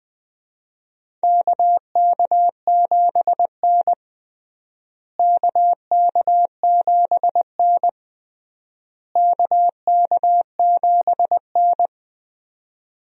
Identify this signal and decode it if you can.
Morse code, heard as a keyed tone, about 21 wpm, 700 Hz. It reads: KK7N KK7N KK7N
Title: USA Call Signs - 4 Characters - Repeated 3x - 20wpm